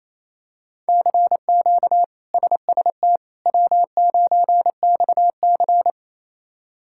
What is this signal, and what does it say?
Morse code, heard as a keyed tone, about 27 wpm, 700 Hz. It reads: CQ SST W9XC
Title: K1USN SST QSO Practice - 28wpm